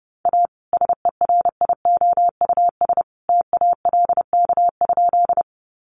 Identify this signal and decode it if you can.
Morse code, heard as a keyed tone, about 30 wpm, 700 Hz. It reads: A SERIOUS TALK?
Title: Sentences from Top 1100 Words 30wpm